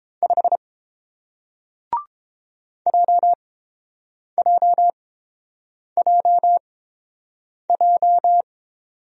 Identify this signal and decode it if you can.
Morse code, heard as a keyed tone, about 30 wpm, spaced as for 7 wpm, 700 Hz. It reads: 5 JJJJ